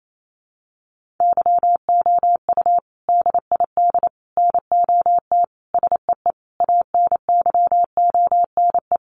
Morse code, heard as a keyed tone, about 28 wpm, 700 Hz. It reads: YOU DID NOT SEE ANYONE
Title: Sentences from Top 900 Words 28wpm